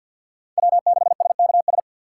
Morse code, heard as a keyed tone, about 64 wpm, 700 Hz. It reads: W6SKS